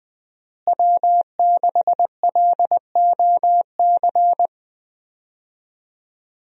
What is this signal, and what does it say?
Morse code, heard as a keyed tone, about 20 wpm, 700 Hz. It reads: W6LOC
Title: USA Call Signs - 5 Characters - Repeated 2x - 20wpm